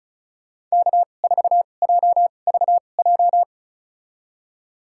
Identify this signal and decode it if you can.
Morse code, heard as a keyed tone, about 34 wpm, spaced as for 25 wpm, 700 Hz. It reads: K4JVJ
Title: USA Call Signs - 5 Characters - 2x Character Spacing - 35wpm